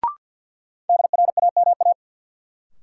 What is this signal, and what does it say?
Morse code, heard as a keyed tone, about 53 wpm, 700 Hz. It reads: DRAMA
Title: Sentences from Top 1500 Words - New Word Drill - 50wpm